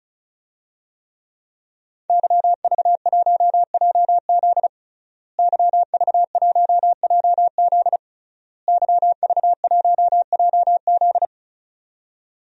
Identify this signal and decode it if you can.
Morse code, heard as a keyed tone, about 36 wpm, 700 Hz. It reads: YV1JZ YV1JZ YV1JZ
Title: DXCC-100 Call Signs - 5 Characters - Repeated 3x - 35wpm